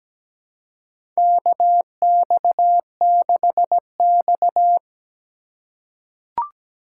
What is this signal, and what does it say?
Morse code, heard as a keyed tone, about 17 wpm, 700 Hz. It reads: KX6X
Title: K1USN SST QSO Practice - 17wpm